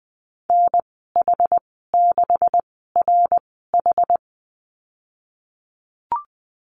Morse code, heard as a keyed tone, about 20 wpm, spaced as for 14 wpm, 700 Hz. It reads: NH6RH
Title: 140 Lesson 23g - Letter f - Call Signs 20wpm